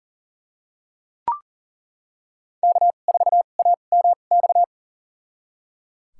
Morse code, heard as a keyed tone, about 37 wpm, spaced as for 28 wpm, 700 Hz. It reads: K4AMX
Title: USA Call Signs - 5 Characters - 2x Character Spacing - 40wpm